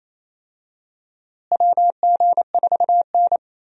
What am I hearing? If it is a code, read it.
Morse code, heard as a keyed tone, about 28 wpm, 700 Hz. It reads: WG4N
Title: USA Call Signs - 4 Characters - Repeated 3x - 28wpm